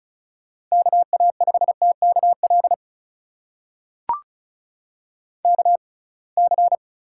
Morse code, heard as a keyed tone, about 36 wpm, spaced as for 27 wpm, 690 Hz. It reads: KA5TKL KC
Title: USA Call Signs - 6 Characters - 6x Character Spacing - 35wpm